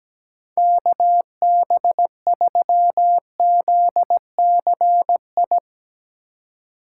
Morse code, heard as a keyed tone, about 18 wpm, 700 Hz. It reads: KB3ZCI
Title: Speed Racing - USA 2x3 Call Signs 17wpm